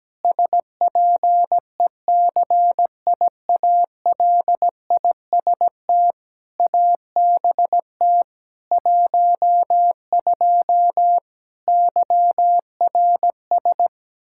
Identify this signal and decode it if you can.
Morse code, heard as a keyed tone, about 17 wpm, 700 Hz. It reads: SPECIALIST ABT 12 YRS
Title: Practice QSO - Exchange 3 - More Info - 17wpm